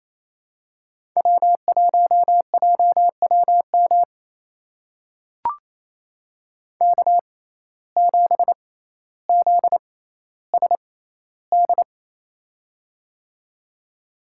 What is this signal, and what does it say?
Morse code, heard as a keyed tone, about 29 wpm, spaced as for 15 wpm, 700 Hz. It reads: W1JWM K7ZSD
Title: USA Call Signs - 6x Character Spacing - 28wpm